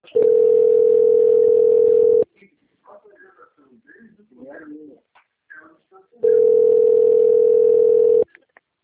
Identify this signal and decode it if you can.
Ringback tone (North America), 440 + 480 Hz, with speech.
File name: out- - - - - .532